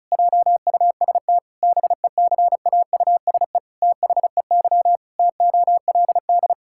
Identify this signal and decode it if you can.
Morse code, heard as a keyed tone, about 36 wpm, 700 Hz. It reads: JUST BECAUSE THEY TOLD
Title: Pangram Challenge v2 - 35wpm